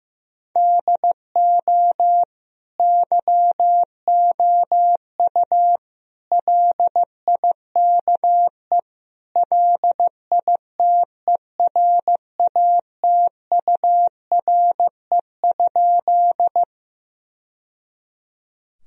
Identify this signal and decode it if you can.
Morse code, heard as a keyed tone, about 16 wpm, 700 Hz. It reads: DO YOU LIKE LITERATURE?
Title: Sentences from Top 1000 Words - Review Part 3 - 15wpm